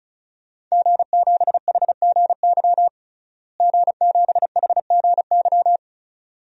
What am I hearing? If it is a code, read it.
Morse code, heard as a keyed tone, about 35 wpm, 700 Hz. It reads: G7HGY G7HGY
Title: DXCC-100 Call Signs - 5 Characters - Repeated 2x - 35wpm